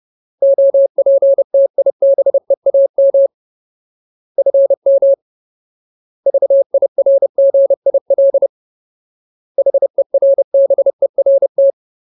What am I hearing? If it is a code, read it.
Morse code, heard as a keyed tone, about 30 wpm, 550 Hz. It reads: OPTIBEAM FM VIRGIL HERBERT
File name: Ultimate Word List / 30WPM Ultimate